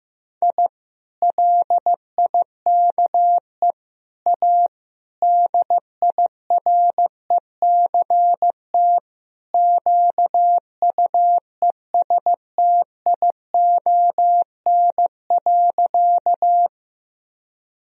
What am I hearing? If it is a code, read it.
Morse code, heard as a keyed tone, about 15 wpm, 700 Hz. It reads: I LIKE A DIRECT QUESTION.
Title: Sentences from Top 1000 Words - Review Part 2 - 15wpm